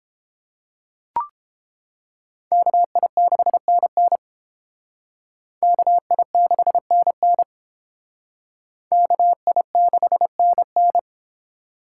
Morse code, heard as a keyed tone, about 28 wpm, 700 Hz. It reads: KI6NN KI6NN KI6NN